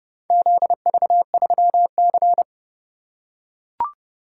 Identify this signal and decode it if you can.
Morse code, heard as a keyed tone, about 30 wpm, 700 Hz. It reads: ZV3C